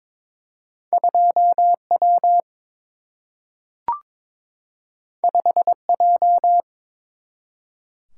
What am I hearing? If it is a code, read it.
Morse code, heard as a keyed tone, about 22 wpm, 700 Hz. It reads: 2 W 5 J
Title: Single-Letter Single-Number 22wpm